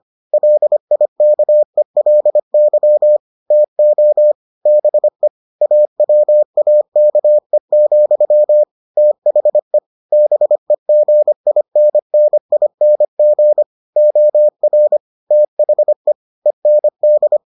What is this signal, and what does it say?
Morse code, heard as a keyed tone, about 25 wpm, 600 Hz. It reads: LIKELY TO BE AWAKE, THE BEGINNING OR THE END